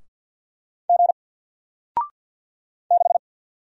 Morse code, heard as a keyed tone, about 50 wpm, 700 Hz. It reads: G 6
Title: Single Letter-Number Alt 50wpm